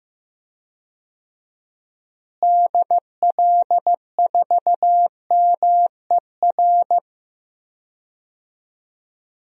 Morse code, heard as a keyed tone, about 15 wpm, 700 Hz. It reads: DL4MER
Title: DXCC-100 Call Signs - 6 Characters - Repeated 2x - 15wpm